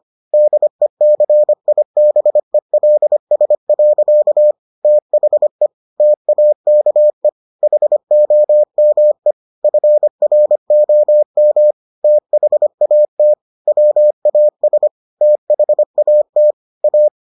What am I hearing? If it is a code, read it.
Morse code, heard as a keyed tone, about 25 wpm, 600 Hz. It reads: DECIBELS. THE TAKE HOME FROM THAT WAS THAT A